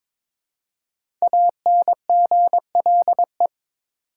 Morse code, heard as a keyed tone, about 22 wpm, 700 Hz. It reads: ANGLE
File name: 5 Letter Words - First 3 Letters Given - 22wpm